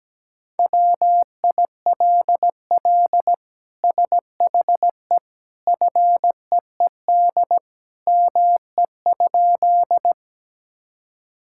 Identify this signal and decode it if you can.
Morse code, heard as a keyed tone, about 17 wpm, 700 Hz. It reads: WILL SHE FEED ME?
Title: Sentences from Top 1600 Words 17wpm